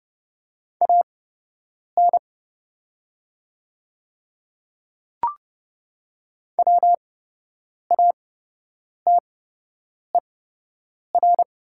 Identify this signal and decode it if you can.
Morse code, heard as a keyed tone, about 30 wpm, spaced as for 8 wpm, 700 Hz. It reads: AN WATER
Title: 8x Character Spacing - Top 100 Words - 30wpm